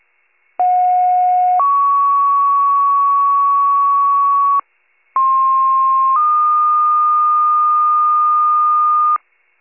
Two-tone page, heard as a one-second tone, then a three-second tone, about 730 then 1120 Hz, about 1060 then 1220 Hz. It is heard from 0.6 to 9.2 s.
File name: pager / 56 RESCUE / wcalert